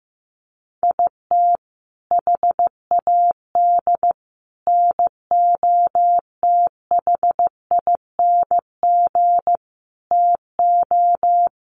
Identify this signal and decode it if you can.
Morse code, heard as a keyed tone, about 15 wpm, 700 Hz. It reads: IT HAD NOTHING TO
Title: Sets of 5 Words Encore2 15wpm